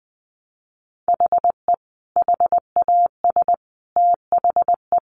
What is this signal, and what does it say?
Morse code, heard as a keyed tone, about 20 wpm, 700 Hz. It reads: HE HAS THE
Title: Sentences from Top 1200 Words 20wpm